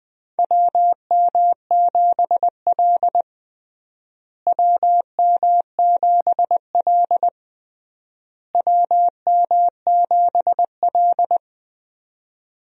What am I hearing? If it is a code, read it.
Morse code, heard as a keyed tone, about 20 wpm, 700 Hz. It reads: WM7L WM7L WM7L
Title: USA Call Signs - 4 Characters - Repeated 3x - 20wpm